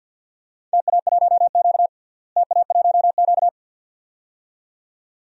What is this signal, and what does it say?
Morse code, heard as a keyed tone, about 53 wpm, 700 Hz. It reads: TA1Q TA1Q
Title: DXCC-100 Call Signs - 4 Characters - Repeated 2x - 50wpm